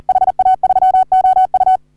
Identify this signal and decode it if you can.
Morse code, heard as a keyed tone, about 37 wpm, 750 Hz. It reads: HA3OU
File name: CW / HA3OU RHOBCW REC